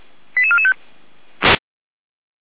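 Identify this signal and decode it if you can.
Five-tone selective call (ZVEI), heard as quick runs of five tones, tones of about 70 ms. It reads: 88445
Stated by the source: ZVEI five-tone selective call received by SDR.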